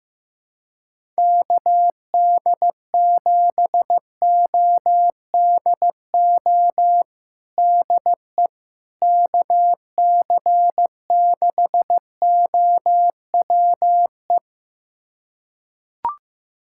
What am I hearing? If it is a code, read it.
Morse code, heard as a keyed tone, about 15 wpm, 700 Hz. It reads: KD7ODO DE KC6OWE
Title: USA Callsign DE Callsign - 6 Characters - 15wpm